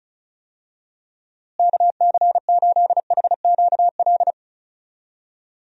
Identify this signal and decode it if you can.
Morse code, heard as a keyed tone, about 36 wpm, 700 Hz. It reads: KC8HQL
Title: USA Call Signs - 6 Characters - Repeated 2x - 35wpm